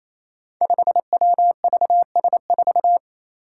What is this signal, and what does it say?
Morse code, heard as a keyed tone, about 28 wpm, 700 Hz. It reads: 5WVS4
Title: Five Challenging Letter-Number Combinations 28wpm